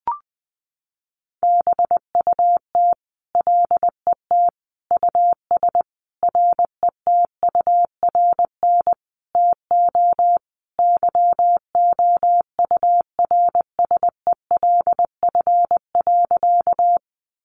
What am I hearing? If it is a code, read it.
Morse code, heard as a keyed tone, about 20 wpm, 700 Hz. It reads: BUT LET US RETURN TO YOURSELF.
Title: Sentences from Top 800 Words 20wpm